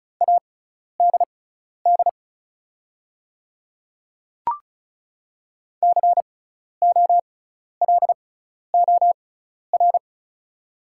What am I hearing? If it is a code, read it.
Morse code, heard as a keyed tone, about 36 wpm, spaced as for 12 wpm, 700 Hz. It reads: ADD COLOR
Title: 6x Character Spacing - Top 500 Words - 35wpm